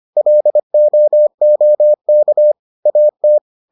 Morse code, heard as a keyed tone, about 25 wpm, 600 Hz. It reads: LOOK AT